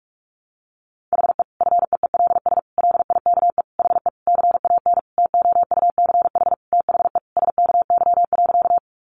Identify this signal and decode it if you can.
Morse code, heard as a keyed tone, about 44 wpm, 700 Hz. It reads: HE FEELS LIKE HE CAN TOUCH THE SKY.
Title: Additional Easy Phrases 45wpm